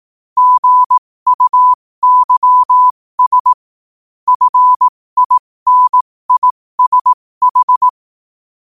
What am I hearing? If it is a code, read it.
Morse code, heard as a keyed tone, about 18 wpm, spaced as for 15 wpm, 1000 Hz. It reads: GUYS FINISH
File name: qotdfiles / 15wpm